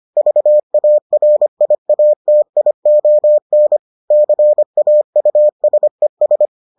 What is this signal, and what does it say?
Morse code, heard as a keyed tone, about 25 wpm, 600 Hz. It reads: VARIATION CAUSES